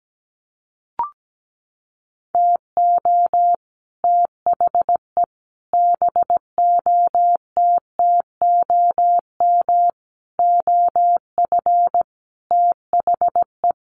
Morse code, heard as a keyed tone, about 17 wpm, 700 Hz. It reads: TO THE BOTTOM OF THE
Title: Sets of 5 Words 17wpm